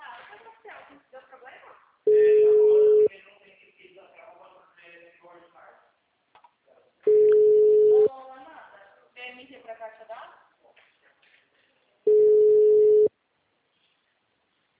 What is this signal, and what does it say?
Ringback tone (Europe, 425 Hz), with speech.